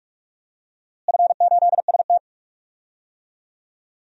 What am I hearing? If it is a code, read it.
Morse code, heard as a keyed tone, about 43 wpm, 700 Hz. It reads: F8ST
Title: DXCC-100 Call Signs - 4 Characters - Repeated 3x - 45wpm